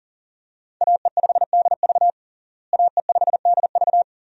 Morse code, heard as a keyed tone, about 40 wpm, 700 Hz. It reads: AE5DV AE5DV
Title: USA Call Signs - 5 Characters - Repeated 2x - 40wpm